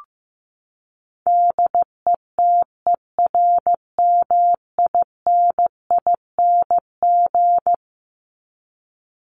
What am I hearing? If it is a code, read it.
Morse code, heard as a keyed tone, about 15 wpm, 700 Hz. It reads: DETERMINING
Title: ING Suffix - Top 100 Words with Base Words 15wpm